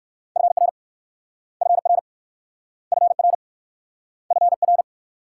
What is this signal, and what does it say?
Morse code, heard as a keyed tone, about 61 wpm, 700 Hz. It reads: FR FR FR FR